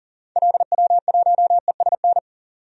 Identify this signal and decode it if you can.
Morse code, heard as a keyed tone, about 40 wpm, 700 Hz. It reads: LW1ESN
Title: DXCC-100 Call Signs - 6 Characters - 40wpm